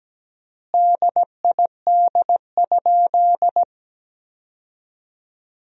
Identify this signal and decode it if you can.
Morse code, heard as a keyed tone, about 17 wpm, 700 Hz. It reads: DID?
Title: 3x Word-Spacing - Sentences from Top 100 Words - 17wpm